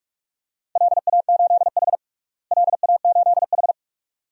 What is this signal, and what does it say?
Morse code, heard as a keyed tone, about 45 wpm, 700 Hz. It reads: LA8H LA8H
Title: DXCC-100 Call Signs - 4 Characters - Repeated 2x - 45wpm